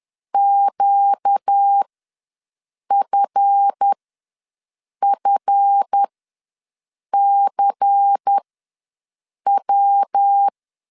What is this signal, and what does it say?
Morse code, heard as a keyed tone, about 10 wpm, spaced as for 6 wpm, 800 Hz. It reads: QFFCW
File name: LEZIONI / 7 GF9Z / 7 GF ws 10wpm 3918